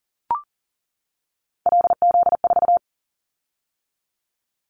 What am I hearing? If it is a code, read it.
Morse code, heard as a keyed tone, about 38 wpm, 700 Hz. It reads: LZ4
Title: Callsign Prefixes for Top 50 DXCC - 40wpm